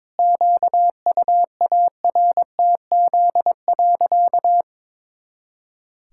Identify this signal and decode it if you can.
Morse code, heard as a keyed tone, about 22 wpm, 700 Hz. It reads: QUARTZ.
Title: Pangram Challenge v2 - 22wpm